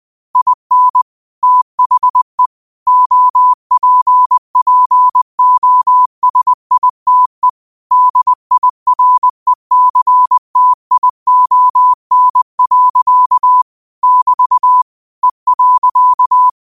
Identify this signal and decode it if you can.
Morse code, heard as a keyed tone, about 20 wpm, 1000 Hz. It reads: IN THE OPPOSITE DIRECTION. <BT> E.
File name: qotdfiles / 20wpm